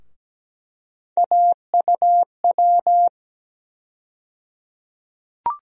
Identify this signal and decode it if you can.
Morse code, heard as a keyed tone, about 17 wpm, 700 Hz. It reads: AUW
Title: USA Call Sign Postfixes - 3 Letters 17wpm